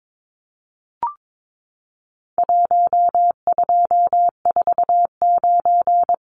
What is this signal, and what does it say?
Morse code, heard as a keyed tone, about 22 wpm, 700 Hz. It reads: 1249
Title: Numbers - 4-Digit 22wpm